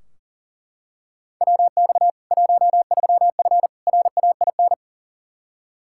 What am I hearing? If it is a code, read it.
Morse code, heard as a keyed tone, about 40 wpm, 700 Hz. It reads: WX 13F RAIN
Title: Practice QSO - Exchange 2 - Info - 40wpm